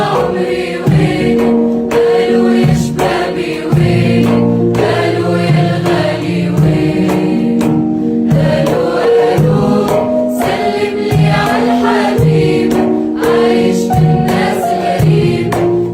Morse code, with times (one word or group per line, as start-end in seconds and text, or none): none